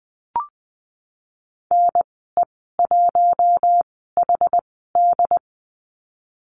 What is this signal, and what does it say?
Morse code, heard as a keyed tone, about 20 wpm, spaced as for 15 wpm, 700 Hz. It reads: NE1HD
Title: 128 Lesson 21g - Number 6 - Call Signs 20wpm